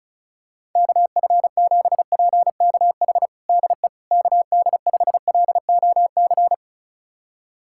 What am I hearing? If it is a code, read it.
Morse code, heard as a keyed tone, about 36 wpm, 700 Hz. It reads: KF7PKH DE KD5LOC
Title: USA Callsign DE Callsign 35wpm